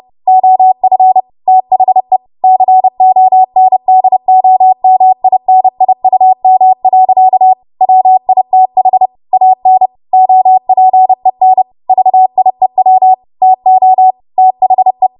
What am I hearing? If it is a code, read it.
Morse code, heard as a keyed tone, about 30 wpm, 750 Hz. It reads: OF THE CONDOMINIUM. WITH AN OPEN VIEW TO THE
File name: Morse / 30WPM